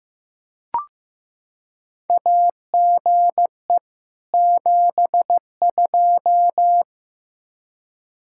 Morse code, heard as a keyed tone, about 15 wpm, 700 Hz. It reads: AGE 72
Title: K1USN SST QSO Practice 15wpm